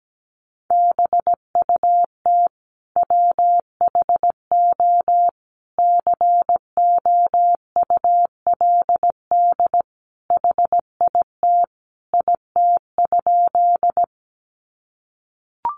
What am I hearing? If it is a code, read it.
Morse code, heard as a keyed tone, about 17 wpm, 700 Hz. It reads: BUT WHO COULD HIT IT?
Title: Sentences from Top 1200 Words 17wpm